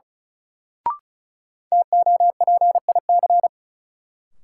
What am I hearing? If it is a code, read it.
Morse code, heard as a keyed tone, about 36 wpm, 700 Hz. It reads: TOPIC
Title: Sentences from Top 1200 Words - New Word Drill Alt - 35wpm